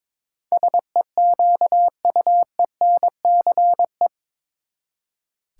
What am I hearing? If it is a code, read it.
Morse code, heard as a keyed tone, about 22 wpm, 700 Hz. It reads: SEQUENCE
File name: practice / 22wpm / Eight Letter Words - 22wpm